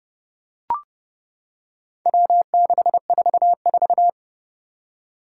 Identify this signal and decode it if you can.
Morse code, heard as a keyed tone, about 31 wpm, 700 Hz. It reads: W644
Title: Four Challenging Letter-Number Combinations 30wpm